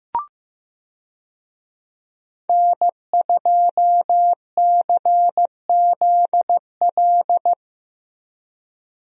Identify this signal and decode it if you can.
Morse code, heard as a keyed tone, about 15 wpm, 700 Hz. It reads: N2CZL
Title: USA Call Signs - 5 Characters - Repeated 2x - 15wpm